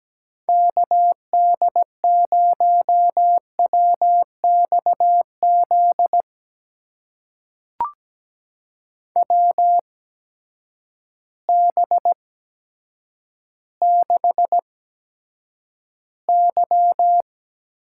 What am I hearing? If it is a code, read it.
Morse code, heard as a keyed tone, about 17 wpm, spaced as for 9 wpm, 700 Hz. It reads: KD0WXZ WB6Y
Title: USA Call Signs - 6 Characters - 8x Character Spacing - 17wpm